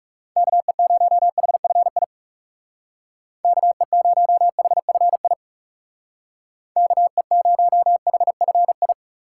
Morse code, heard as a keyed tone, about 40 wpm, 700 Hz. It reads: KE0HFI KE0HFI KE0HFI